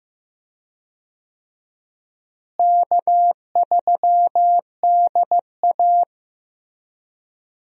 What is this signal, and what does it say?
Morse code, heard as a keyed tone, about 15 wpm, 700 Hz. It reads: K3DA
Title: USA Call Signs - 4 Characters - Repeated 3x - 15wpm